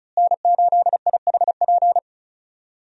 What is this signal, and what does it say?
Morse code, heard as a keyed tone, about 36 wpm, 700 Hz. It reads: N8IHP
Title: USA Call Signs - 5 Characters - Repeated 3x - 35wpm